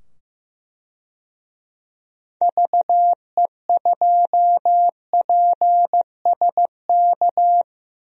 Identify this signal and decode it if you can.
Morse code, heard as a keyed tone, about 15 wpm, 700 Hz. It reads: VE2PSK
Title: DXCC-100 Call Signs - 6 Characters - 15wpm